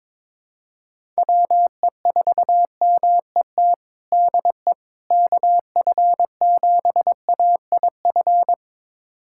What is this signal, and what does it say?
Morse code, heard as a keyed tone, about 23 wpm, 700 Hz. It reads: WE4MET DE KF7AIF
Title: USA Callsign DE Callsign - 6 Characters - 22wpm